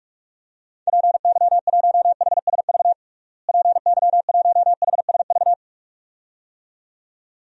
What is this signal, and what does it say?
Morse code, heard as a keyed tone, about 46 wpm, 700 Hz. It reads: PY1HSV PY1HSV
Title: DXCC-100 Call Signs - 6 Characters - Repeated 2x - 45wpm